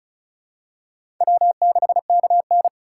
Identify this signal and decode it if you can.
Morse code, heard as a keyed tone, about 36 wpm, 700 Hz. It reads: W6KN
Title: USA Call Signs - 3x Character Spacing - 35wpm